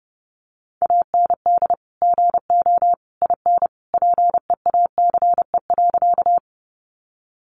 Morse code, heard as a keyed tone, about 30 wpm, 700 Hz. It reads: AND GO IN PEACE.
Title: Sentences from Top 1500 Words 30wpm